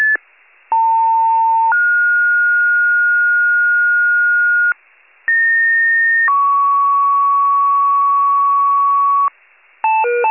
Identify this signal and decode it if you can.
Two-tone page, heard as a one-second tone, then a three-second tone, about 900 then 1430 Hz, about 1850 then 1120 Hz, with noise. It is heard from 0.7 to 9.3 s.
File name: pager / COUNTY EMS / wcalert